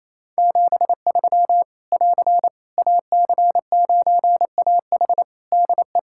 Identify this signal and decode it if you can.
Morse code, heard as a keyed tone, about 28 wpm, 700 Hz. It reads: 73 <AR> AC9AH DE
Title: Practice QSO - Exchange 4 - Ending - 28wpm